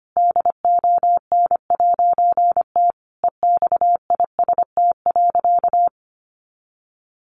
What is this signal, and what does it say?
Morse code, heard as a keyed tone, about 25 wpm, 700 Hz. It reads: DON'T EXIST.
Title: A Sound Of Thunder 25wpm